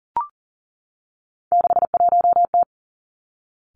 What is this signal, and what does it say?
Morse code, heard as a keyed tone, about 38 wpm, 700 Hz. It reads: 61T
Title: Three Challenging Letter-Number Combinations 40wpm